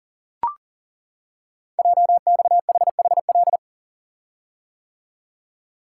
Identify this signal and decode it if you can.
Morse code, heard as a keyed tone, about 42 wpm, 700 Hz. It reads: JXHHL
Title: Code Groups - 5 Letters - 40wpm